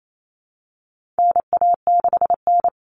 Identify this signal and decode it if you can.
Morse code, heard as a keyed tone, about 28 wpm, 700 Hz. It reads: NA6N
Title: Common Cut Numbers - 4-Digit 28wpm